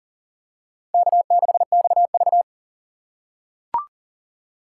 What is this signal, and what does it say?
Morse code, heard as a keyed tone, about 40 wpm, 700 Hz. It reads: K6XV
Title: USA Call Signs - 4 Characters - 2x Character Spacing - 40wpm